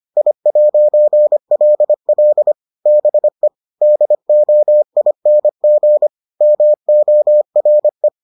Morse code, heard as a keyed tone, about 25 wpm, 600 Hz. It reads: I'LL BE DOING MORE